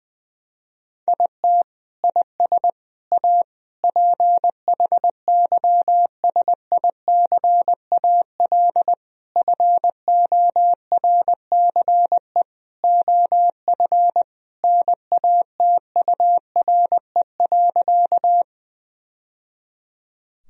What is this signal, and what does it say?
Morse code, heard as a keyed tone, about 20 wpm, 700 Hz. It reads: IT IS A PHYSICAL FORCE OF NATURE.
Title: Sentences from Top 1000 Words - Review Part 4 - 20wpm